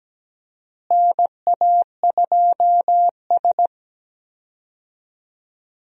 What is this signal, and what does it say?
Morse code, heard as a keyed tone, about 17 wpm, 700 Hz. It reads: NA2S
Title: USA Call Signs - 4 Characters - Repeated 3x - 17wpm